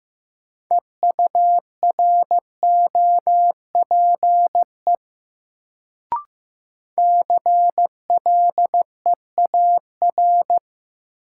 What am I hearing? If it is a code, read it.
Morse code, heard as a keyed tone, about 15 wpm, 700 Hz. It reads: EUROPE CLEAR
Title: Sets of 5 Words Encore - New Word Drill - 15wpm